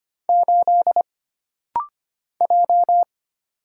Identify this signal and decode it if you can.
Morse code, heard as a keyed tone, about 25 wpm, 700 Hz. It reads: 8 J
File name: Single Character (Letter:Number:Punctuation:Prosign) - Rapid-Fire - 25wpm